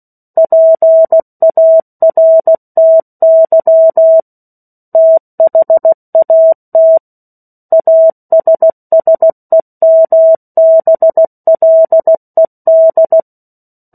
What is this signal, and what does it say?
Morse code, heard as a keyed tone, about 16 wpm, 650 Hz. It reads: PARTY THAT ASSEMBLED
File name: 03-Chapter-03 16 WPM